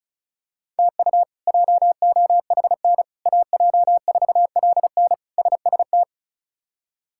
Morse code, heard as a keyed tone, about 34 wpm, 700 Hz. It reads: TU JOHN AJ4LN SST
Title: K1USN SST QSO Practice - 35wpm